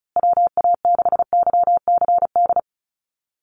Morse code, heard as a keyed tone, about 34 wpm, 700 Hz. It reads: WA6YCD
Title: USA Call Signs Repeated 2x 35wpm